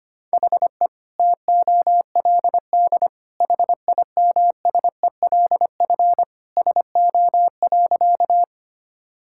Morse code, heard as a keyed tone, about 25 wpm, 700 Hz. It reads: HE TOLD HIMSELF SO.
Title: Sentences from Top 1600 Words 25wpm